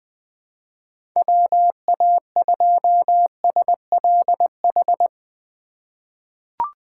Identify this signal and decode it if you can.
Morse code, heard as a keyed tone, about 21 wpm, 700 Hz. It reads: WA2SLH
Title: 2023 POTA Hunter Call Signs - 20wpm